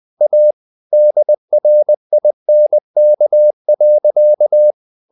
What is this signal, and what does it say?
Morse code, heard as a keyed tone, about 20 wpm, 600 Hz. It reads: A DRINK.
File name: the old man and the sea part 2 20wpm